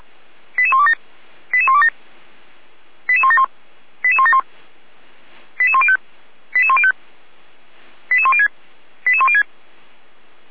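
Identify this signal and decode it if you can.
Five-tone selective call (ZVEI), heard as quick runs of five tones, tones of about 70 ms. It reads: 88127 88127 88171 88171 88105 88105 88106 88106
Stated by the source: ZVEI five-tone call 88102 received via SDR, Feuerwehr Orlamünde fire brigade.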